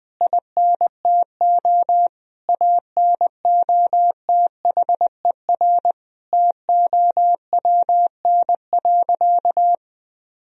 Morse code, heard as a keyed tone, about 20 wpm, 700 Hz. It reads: INTO ANOTHER TOWN.
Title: Sentences from Top 1000 Words - Review Part 2 - 20wpm